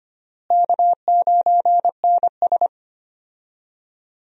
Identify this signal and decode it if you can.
Morse code, heard as a keyed tone, about 25 wpm, 700 Hz. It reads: K9NS
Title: USA Call Signs - 4 Characters - Repeated 3x - 25wpm